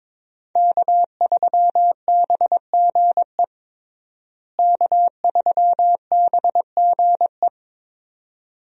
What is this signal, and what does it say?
Morse code, heard as a keyed tone, about 21 wpm, 700 Hz. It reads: K3BGE K3BGE
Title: USA Call Signs - 5 Characters - Repeated 2x - 22wpm